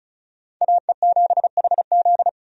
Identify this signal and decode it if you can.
Morse code, heard as a keyed tone, about 36 wpm, 700 Hz. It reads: AE7HZ
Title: USA Call Signs - 5 Characters - 35wpm